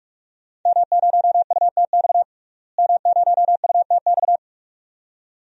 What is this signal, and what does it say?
Morse code, heard as a keyed tone, about 45 wpm, 700 Hz. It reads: M0UTX M0UTX
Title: DXCC-100 Call Signs - 5 Characters - Repeated 2x - 45wpm